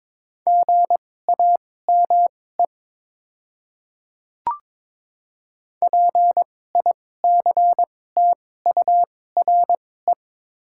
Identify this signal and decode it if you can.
Morse code, heard as a keyed tone, about 22 wpm, spaced as for 16 wpm, 700 Hz. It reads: GAME PICTURE
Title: 2x Character Spacing - Top 500 Words - 22wpm